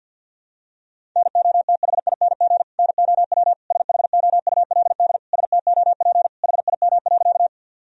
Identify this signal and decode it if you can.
Morse code, heard as a keyed tone, about 49 wpm, 700 Hz. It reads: NOTHING NOW SHOULD STOP HIM.
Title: Sentences from Top 1000 Words - Review Part 1 - 50wpm